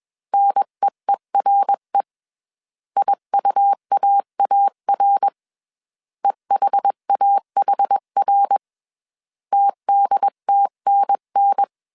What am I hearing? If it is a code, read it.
Morse code, heard as a keyed tone, about 21 wpm, 800 Hz. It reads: NEERE IUAAR EHAHR TDTNN
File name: CW / DET / 1 3 1500C S B 20wpm 1754